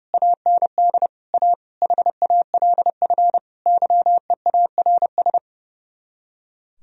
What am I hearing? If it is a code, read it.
Morse code, heard as a keyed tone, about 37 wpm, 700 Hz. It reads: AND A HALF YEARS
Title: Speed-Racing - Sets of 5 Words 30wpm